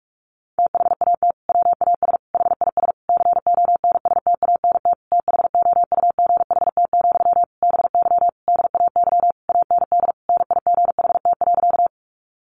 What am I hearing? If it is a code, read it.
Morse code, heard as a keyed tone, about 44 wpm, 700 Hz. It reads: THAT WAS HIS CONSTANT THOUGHT, BY DAY AND NIGHT.
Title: Sentences from Top 1500 Words 45wpm